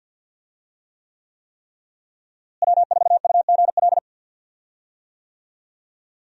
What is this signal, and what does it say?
Morse code, heard as a keyed tone, about 52 wpm, 700 Hz. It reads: W4UGL
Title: USA Call Signs - 5 Characters - Repeated 3x - 50wpm